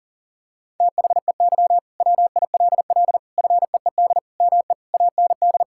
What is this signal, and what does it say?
Morse code, heard as a keyed tone, about 40 wpm, 700 Hz. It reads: THEY WILL FEED ME AND
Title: Sentences from Top 1600 Words 40wpm